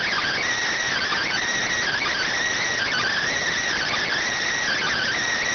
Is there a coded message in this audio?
Yes: Emergency Alert System (SAME) tones (header bursts) from 0.9 to 5.2 s, heard as harsh buzzing bursts.